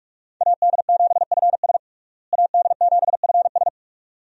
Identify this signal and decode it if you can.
Morse code, heard as a keyed tone, about 46 wpm, 700 Hz. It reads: AD7FS AD7FS
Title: USA Call Signs - 5 Characters - Repeated 2x - 45wpm